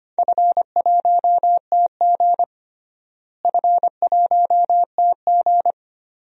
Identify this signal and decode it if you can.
Morse code, heard as a keyed tone, about 25 wpm, 700 Hz. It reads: F1TG F1TG
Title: DXCC-100 Call Signs - 4 Characters - Repeated 2x - 25wpm